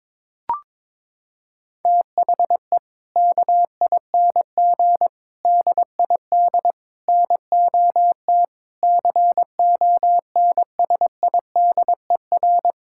Morse code, heard as a keyed tone, about 22 wpm, 700 Hz. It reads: THE KING DID NOT CONSIDER
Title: Sentences from Top 1000 Words - Review Part 5 - 22wpm